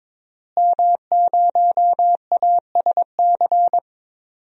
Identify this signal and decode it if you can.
Morse code, heard as a keyed tone, about 22 wpm, 700 Hz. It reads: M0ASC
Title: DXCC-100 Call Signs - 5 Characters - Repeated 2x - 22wpm